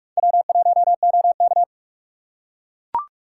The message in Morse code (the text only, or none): W1OK